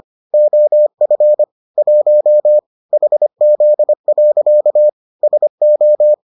Morse code, heard as a keyed tone, about 25 wpm, 600 Hz. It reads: OF 1 HZ. SO